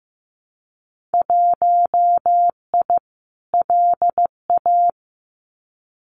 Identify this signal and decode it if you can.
Morse code, heard as a keyed tone, about 15 wpm, 700 Hz. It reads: 1I LA
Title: Winter Field Day Exchanges Encore 15wpm